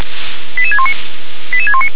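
Five-tone selective call (ZVEI), heard as quick runs of five tones, tones of about 70 ms. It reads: 88619 88619
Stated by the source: ZVEI five-tone selective call received by SDR.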